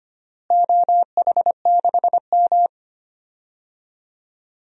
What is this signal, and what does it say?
Morse code, heard as a keyed tone, about 25 wpm, 700 Hz. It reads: OH6M
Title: DXCC-100 Call Signs - 4 Characters - Repeated 3x - 25wpm